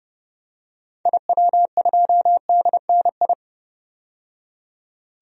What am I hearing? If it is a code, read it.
Morse code, heard as a keyed tone, about 31 wpm, 700 Hz. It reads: IW2DNI
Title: DXCC-100 Call Signs - 6 Characters - 30wpm